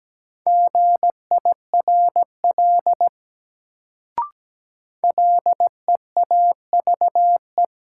Morse code, heard as a keyed tone, about 17 wpm, 700 Hz. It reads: GIRL LEAVE
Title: Sentences from Top 400 Words - New Word Drill - 17wpm